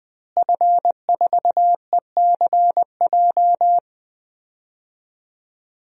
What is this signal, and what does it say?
Morse code, heard as a keyed tone, about 20 wpm, 700 Hz. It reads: F4ECJ
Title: DXCC-100 Call Signs - 5 Characters - Repeated 2x - 20wpm